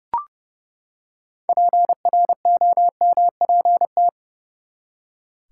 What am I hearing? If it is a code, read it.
Morse code, heard as a keyed tone, about 30 wpm, 700 Hz. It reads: PROMPT
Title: Six Letter Words - 30wpm